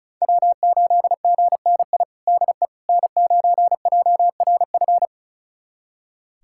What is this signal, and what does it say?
Morse code, heard as a keyed tone, about 36 wpm, 700 Hz. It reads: W8GNI DE N9JRF
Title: USA Callsign DE Callsign - 5 Characters - 35wpm